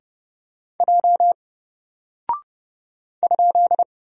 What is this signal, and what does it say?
Morse code, heard as a keyed tone, about 29 wpm, 700 Hz. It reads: J ?
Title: Single Character (Letter:Number:Punctuation:Prosign) 30wpm